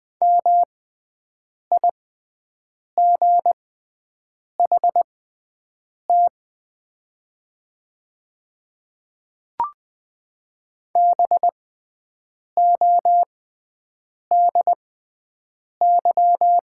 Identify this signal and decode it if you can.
Morse code, heard as a keyed tone, about 20 wpm, spaced as for 7 wpm, 700 Hz. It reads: MIGHT BODY
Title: 6x Character Spacing - Top 500 Words - 20wpm